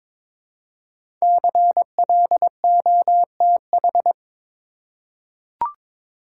Morse code, heard as a keyed tone, about 22 wpm, 700 Hz. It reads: CLOTH
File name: 5 Letter Words - First 3 Letters Given - 22wpm